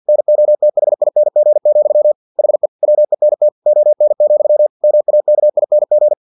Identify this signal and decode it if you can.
Morse code, heard as a keyed tone, about 50 wpm, 600 Hz. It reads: NOTHING, HE WENT ON, MAKING
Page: Never 49wpm